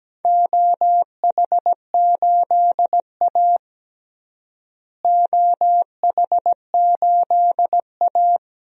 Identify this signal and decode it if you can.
Morse code, heard as a keyed tone, about 17 wpm, 700 Hz. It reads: OH8A OH8A
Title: DXCC-100 Call Signs - 4 Characters - Repeated 2x - 17wpm